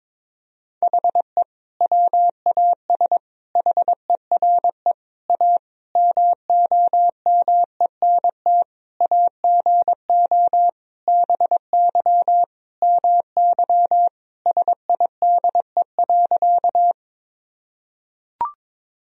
Morse code, heard as a keyed tone, about 22 wpm, 700 Hz. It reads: HE WAS HERE A MOMENT AGO BY MY SIDE.
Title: Sentences from Top 1000 Words - Review Part 3 - 22wpm